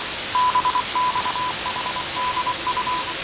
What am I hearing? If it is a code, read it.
Morse code, heard as a keyed tone, about 24 wpm, 1000 Hz, with noise. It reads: BXHDU
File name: 6824usb